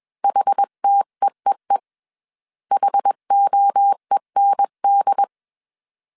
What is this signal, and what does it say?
Morse code, heard as a keyed tone, about 21 wpm, 800 Hz. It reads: HTEEE HOEND
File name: 20wpm